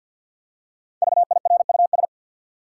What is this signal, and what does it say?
Morse code, heard as a keyed tone, about 48 wpm, 700 Hz. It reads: VIRUS
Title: Five Letter Words - 50wpm